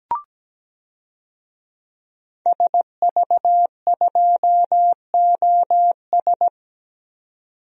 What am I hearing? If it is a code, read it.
Morse code, heard as a keyed tone, about 17 wpm, 700 Hz. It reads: SV2OS
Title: DXCC-100 Call Signs - 5 Characters - 17wpm